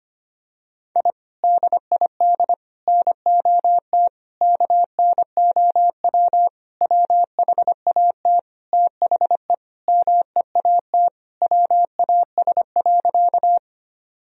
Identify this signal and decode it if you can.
Morse code, heard as a keyed tone, about 25 wpm, 700 Hz. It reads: I DID NOT KNOW WHAT THE MEAT WAS.
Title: Sentences from Top 1000 Words - Review Part 3 - 25wpm